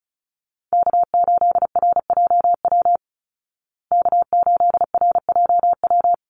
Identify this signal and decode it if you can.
Morse code, heard as a keyed tone, about 34 wpm, 700 Hz. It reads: K8RJW K8RJW
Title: Call Signs Repeated 3x - 35wpm